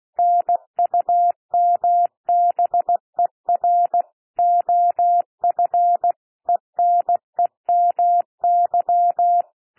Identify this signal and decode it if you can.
Morse code, heard as a keyed tone, about 16 wpm, 700 Hz. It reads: NUMBER OF ENEMY
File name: cwmp3 / P1 SOUND PASSAGE 16 wpm